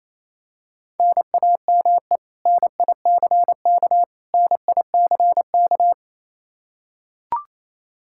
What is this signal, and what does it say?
Morse code, heard as a keyed tone, about 28 wpm, 700 Hz. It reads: NAME NICK NICK
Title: Practice QSO - Exchange 1 - Beginning - 28wpm